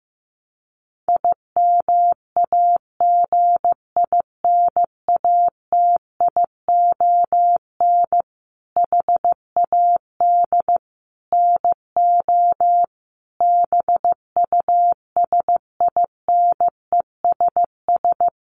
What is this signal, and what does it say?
Morse code, heard as a keyed tone, about 14 wpm, 700 Hz. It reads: IMAGINATION HAD NO BUSINESS
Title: Sentences from Top 1300 Words 15wpm